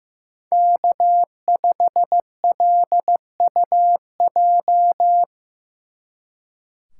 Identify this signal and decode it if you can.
Morse code, heard as a keyed tone, about 17 wpm, 700 Hz. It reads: K5LUJ
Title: Speed Racing - USA Call Signs 15wpm